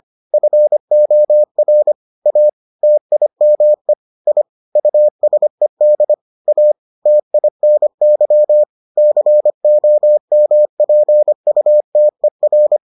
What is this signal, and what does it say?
Morse code, heard as a keyed tone, about 25 wpm, 600 Hz. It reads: FOR A TIME I USED A TINY COMPUTER